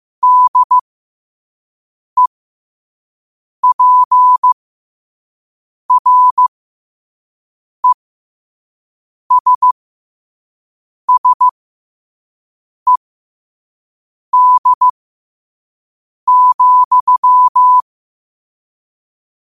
Morse code, heard as a keyed tone, about 15 wpm, spaced as for 5 wpm, 1000 Hz. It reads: DEPRESSED,
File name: qotdfiles / 5wpm